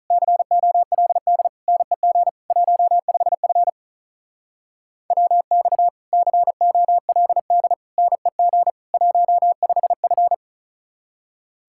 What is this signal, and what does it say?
Morse code, heard as a keyed tone, about 46 wpm, 700 Hz. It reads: COLD NEG 15F WX COLD NEG 15F